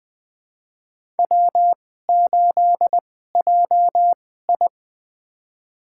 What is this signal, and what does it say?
Morse code, heard as a keyed tone, about 21 wpm, spaced as for 15 wpm, 700 Hz. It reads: W8JI
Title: USA Call Signs - 2x Character Spacing - 20wpm